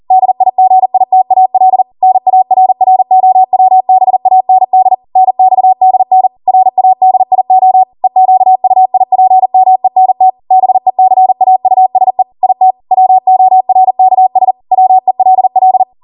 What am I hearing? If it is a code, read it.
Morse code, heard as a keyed tone, about 40 wpm, 750 Hz. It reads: DIGITAL NARROWBAND NXDN RADIO EQUIPMENT BECAUSE IT WORKS WELL